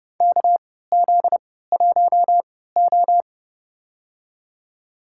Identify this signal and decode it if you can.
Morse code, heard as a keyed tone, about 30 wpm, spaced as for 17 wpm, 700 Hz. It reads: KZ1O
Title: USA Call Signs - 4 Characters - 3x Character Spacing - 30wpm